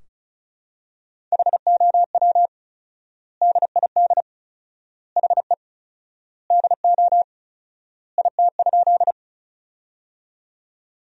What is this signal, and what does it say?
Morse code, heard as a keyed tone, about 34 wpm, 700 Hz. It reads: HOW DID HE DO IT?
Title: 4x Word-Spacing - Sentences from Top 100 Words - 35wpm